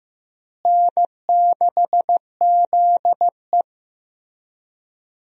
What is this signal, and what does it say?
Morse code, heard as a keyed tone, about 15 wpm, 700 Hz. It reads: N6ZE
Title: USA Call Signs - 4 Characters - Repeated 3x - 15wpm